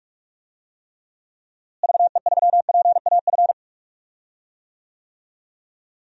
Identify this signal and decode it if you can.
Morse code, heard as a keyed tone, about 44 wpm, 700 Hz. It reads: VE3PAF
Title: DXCC-100 Call Signs - 6 Characters - Repeated 3x - 45wpm